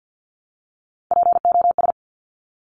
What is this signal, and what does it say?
Morse code, heard as a keyed tone, about 50 wpm, 700 Hz. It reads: POS
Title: Top Three Letter Combinations in Words 50wpm